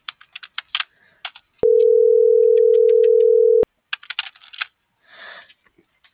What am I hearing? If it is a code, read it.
Ringback tone (North America), 440 + 480 Hz.